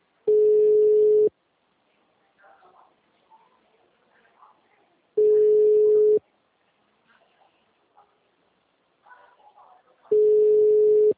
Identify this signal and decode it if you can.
Ringback tone (Europe, 425 Hz).